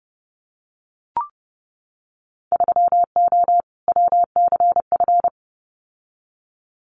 Morse code, heard as a keyed tone, about 31 wpm, 700 Hz. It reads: 3O WCF
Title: Winter Field Day Exchanges Encore 30wpm